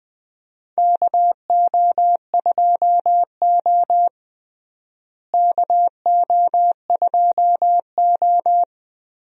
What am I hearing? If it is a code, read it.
Morse code, heard as a keyed tone, about 20 wpm, 700 Hz. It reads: KO2O KO2O
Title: USA Call Signs - 4 Characters - Repeated 2x - 20wpm